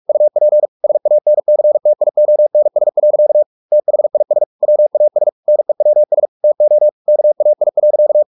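Morse code, heard as a keyed tone, about 44 wpm, 600 Hz. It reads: UP SANCTIONS. THIS WAS NEWS TO KAI.
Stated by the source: Morse practice at 45 wpm, the novel Never by Ken Follett in English, from CW med Gnister.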